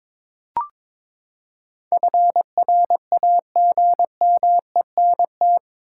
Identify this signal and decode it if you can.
Morse code, heard as a keyed tone, about 22 wpm, 700 Hz. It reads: FRAGMENT
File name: practice / 22wpm / Eight Letter Words - 22wpm